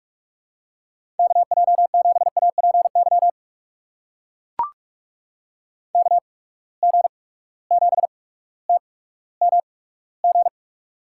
Morse code, heard as a keyed tone, about 42 wpm, spaced as for 19 wpm, 700 Hz. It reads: KJ7APY KG7TMG
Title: USA Call Signs - 8x Character Spacing - 45wpm